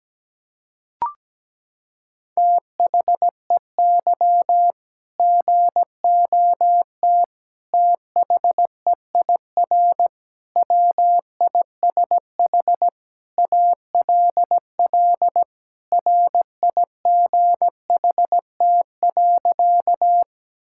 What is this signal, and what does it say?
Morse code, heard as a keyed tone, about 17 wpm, 700 Hz. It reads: THEY GOT THEIR WISH ALL RIGHT.
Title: Sentences from Top 1600 Words 17wpm